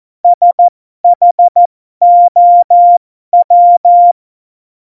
Morse code, heard as a keyed tone, about 14 wpm, spaced as for 12 wpm, 700 Hz. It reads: SHOW
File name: Morsekurs / Witze